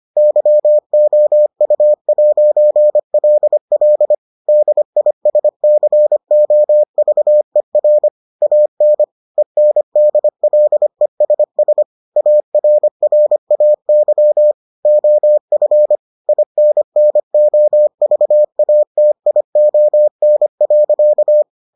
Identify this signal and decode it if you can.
Morse code, heard as a keyed tone, about 25 wpm, 600 Hz. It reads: YOU'LL DISCOVER AN ENDLESS ARRAY OF INNOVATION.